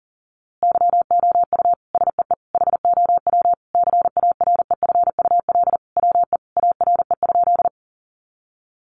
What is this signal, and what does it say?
Morse code, heard as a keyed tone, about 40 wpm, 700 Hz. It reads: YOU SEE HOW CAREFUL WE ARE?
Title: A Sound Of Thunder 40wpm